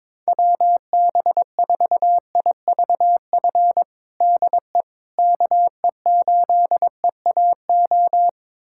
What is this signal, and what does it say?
Morse code, heard as a keyed tone, about 22 wpm, 700 Hz. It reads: WB4IVF DE KE8EAO
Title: USA Callsign DE Callsign 22wpm